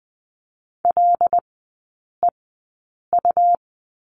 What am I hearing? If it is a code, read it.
Morse code, heard as a keyed tone, about 20 wpm, spaced as for 8 wpm, 700 Hz. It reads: LEU
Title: 100 Lesson 17e - Letter u - Three Characters 20wpm